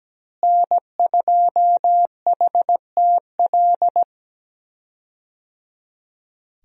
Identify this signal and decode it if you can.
Morse code, heard as a keyed tone, about 17 wpm, 700 Hz. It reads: N2HTL
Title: USA Call Signs - 5 Characters - Repeated 3x - 17wpm